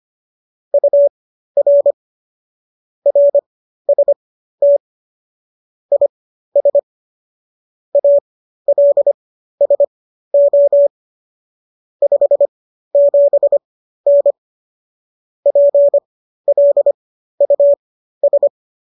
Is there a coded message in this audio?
Yes: Morse code from 0.7 to 18.5 s, heard as a keyed tone, about 25 wpm, spaced as for 13 wpm, 570 Hz. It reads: UR RST IS ALSO 57N PLUS